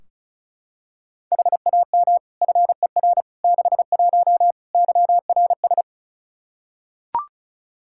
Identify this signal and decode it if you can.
Morse code, heard as a keyed tone, about 36 wpm, 700 Hz. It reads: HAM FER 61 YRS
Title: K1USN SST QSO Practice 35wpm